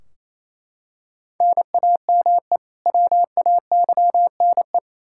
Morse code, heard as a keyed tone, about 27 wpm, 700 Hz. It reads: NAME WAYNE
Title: Practice QSO - Exchange 1 - Beginning - 28wpm